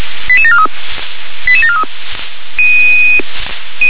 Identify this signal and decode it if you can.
Five-tone selective call (ZVEI), heard as quick runs of five tones, tones of about 70 ms. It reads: 88743 88743